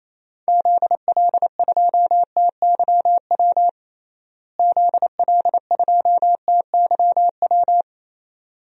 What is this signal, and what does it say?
Morse code, heard as a keyed tone, about 28 wpm, 700 Hz. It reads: ZL2TYW ZL2TYW
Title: DXCC-100 Call Signs - 6 Characters - Repeated 2x - 28wpm